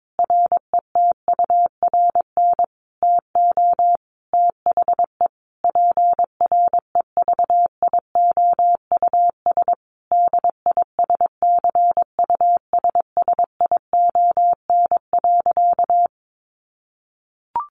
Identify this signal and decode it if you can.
Morse code, heard as a keyed tone, about 22 wpm, 700 Hz. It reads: RETURN TO THE PREVIOUS DISCUSSION.
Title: Sentences from Top 1500 Words 22wpm